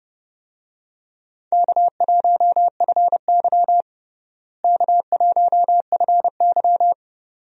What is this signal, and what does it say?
Morse code, heard as a keyed tone, about 32 wpm, 700 Hz. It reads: K1FY K1FY
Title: USA Call Signs - 4 Characters - Repeated 2x - 30wpm